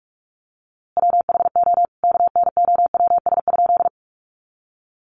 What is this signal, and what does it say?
Morse code, heard as a keyed tone, about 44 wpm, 700 Hz. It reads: WHO KNOWS?
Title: A Sound Of Thunder 45wpm